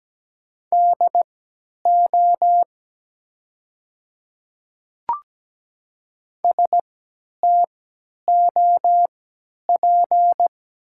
Morse code, heard as a keyed tone, about 17 wpm, spaced as for 10 wpm, 700 Hz. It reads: DO STOP
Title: 3x Character Spacing - Top 300 Words - 17wpm